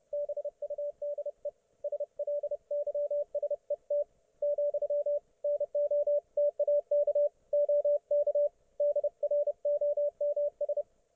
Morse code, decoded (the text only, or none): BUDE SLYSET, NOTAK OK DROMS